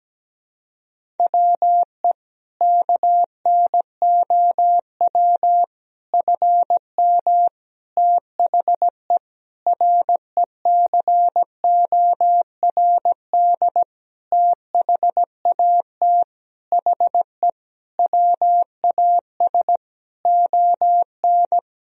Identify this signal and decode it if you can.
Morse code, heard as a keyed tone, about 17 wpm, 700 Hz. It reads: WE KNOW FM THE RECORD THAT HE WAS ON
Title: Sentences from Top 500 Words and Abbreviations 17wpm